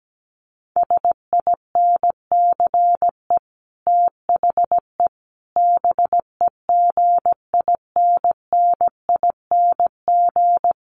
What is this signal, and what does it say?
Morse code, heard as a keyed tone, about 17 wpm, 700 Hz. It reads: SINCE THE BEGINNING
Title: Sets of 5 Words Encore2 17wpm